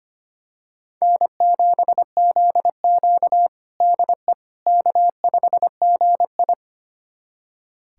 Morse code, heard as a keyed tone, about 25 wpm, 700 Hz. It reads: N7ZQ DE K5GI
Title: USA Callsign DE Callsign - 4 Characters - 25wpm